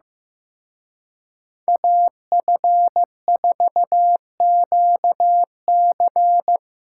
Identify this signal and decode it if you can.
Morse code, heard as a keyed tone, about 18 wpm, 700 Hz. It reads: AF4QC
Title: Speed Racing - USA 2x2 Call Signs 15wpm